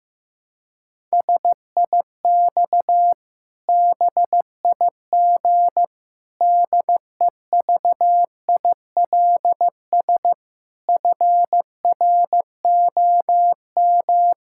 Morse code, heard as a keyed tone, about 15 wpm, 700 Hz. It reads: SIX BIG DEVILS FROM
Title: Pangram Challenge v2 - 15wpm